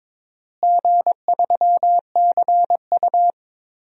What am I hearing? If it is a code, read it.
Morse code, heard as a keyed tone, about 22 wpm, 700 Hz. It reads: G3CU
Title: DXCC-100 Call Signs - 4 Characters - Repeated 3x - 22wpm